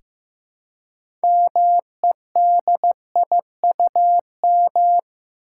Morse code, heard as a keyed tone, about 15 wpm, 700 Hz. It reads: MEDIUM
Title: Six Letter Words - 15wpm